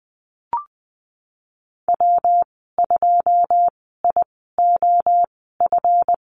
Morse code, heard as a keyed tone, about 20 wpm, spaced as for 15 wpm, 700 Hz. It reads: W2IOF
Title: 161 Lesson 26g - Letter g - Call Signs 20wpm